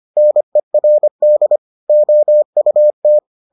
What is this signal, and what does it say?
Morse code, heard as a keyed tone, about 25 wpm, 600 Hz. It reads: NERD OUT